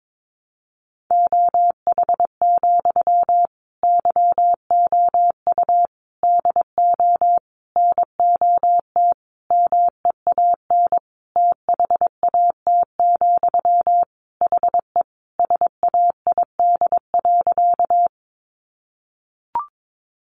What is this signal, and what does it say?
Morse code, heard as a keyed tone, about 22 wpm, 700 Hz. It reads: OH, YOU DO NOT MEAN THAT, HE SAID.
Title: Sentences from Top 600 Words 22wpm